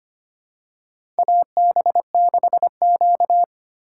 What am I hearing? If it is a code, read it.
Morse code, heard as a keyed tone, about 25 wpm, 690 Hz. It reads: AB6Q
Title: USA Call Signs - 4 Characters - 4x Character Spacing - 25wpm